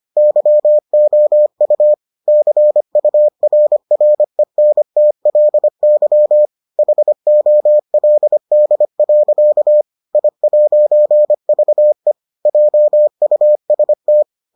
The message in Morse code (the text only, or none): YOU CURRENTLY HOLD. I'VE JUST